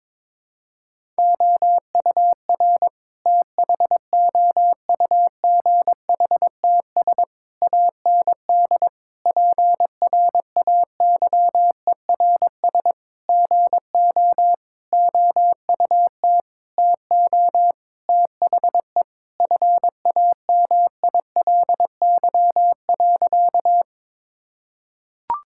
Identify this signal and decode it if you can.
Morse code, heard as a keyed tone, about 22 wpm, 700 Hz. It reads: OUR THOUGHTS AND PRAYERS GO OUT TO THE FAMILY.
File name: practice / 22wpm / Sentences with 8-Word Phrases - 22wpm